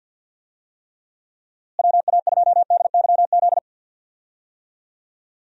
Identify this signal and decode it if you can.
Morse code, heard as a keyed tone, about 46 wpm, 700 Hz. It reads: WA2DYZ
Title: USA Call Signs - 6 Characters - Repeated 3x - 50wpm